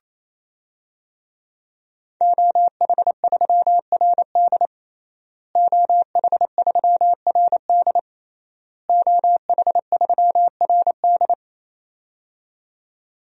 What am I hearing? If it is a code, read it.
Morse code, heard as a keyed tone, about 28 wpm, 700 Hz. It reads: OH3RD OH3RD OH3RD
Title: DXCC-100 Call Signs - 5 Characters - Repeated 3x - 28wpm